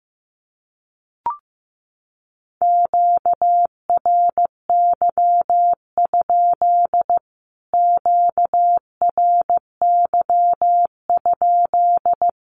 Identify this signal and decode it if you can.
Morse code, heard as a keyed tone, about 15 wpm, 700 Hz. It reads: QRY? QRY?
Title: Q-Codes Repeated 3x 15wpm